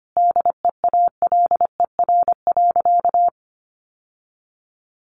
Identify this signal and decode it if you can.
Morse code, heard as a keyed tone, about 25 wpm, 700 Hz. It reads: DEALER.
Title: Sentences from Top 1100 Words 25wpm